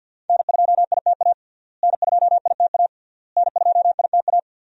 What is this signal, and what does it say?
Morse code, heard as a keyed tone, about 51 wpm, 700 Hz. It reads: N2ITA N2ITA N2ITA
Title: USA Call Signs - 5 Characters - Repeated 3x - 50wpm